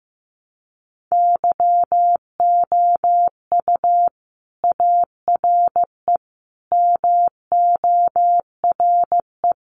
Morse code, heard as a keyed tone, about 15 wpm, 700 Hz. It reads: YOU ARE MORE
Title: Sentences from Top 1300 Words 15wpm